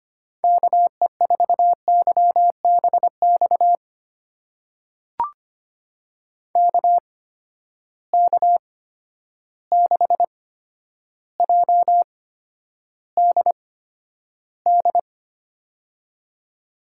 Morse code, heard as a keyed tone, about 25 wpm, spaced as for 11 wpm, 700 Hz. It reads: KE4YBX KK6JDD
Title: USA Call Signs - 6 Characters - 8x Character Spacing - 25wpm